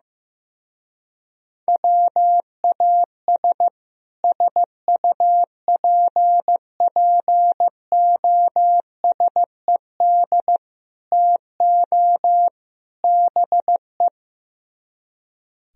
Morse code, heard as a keyed tone, about 16 wpm, 700 Hz. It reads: WAS SUPPOSED TO BE
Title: Speed-Racing - Sets of 4 Words 15wpm